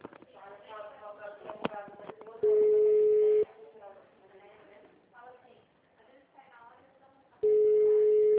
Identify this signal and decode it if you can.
Ringback tone (Europe, 425 Hz), with speech.
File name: out- -2014- - - .21249